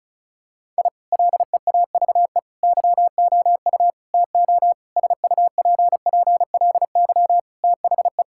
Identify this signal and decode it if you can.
Morse code, heard as a keyed tone, about 34 wpm, 700 Hz. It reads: I LEAVE YOU TO SUPPLY THE
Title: Sentences from Top 1000 Words - Review Part 5 - 35wpm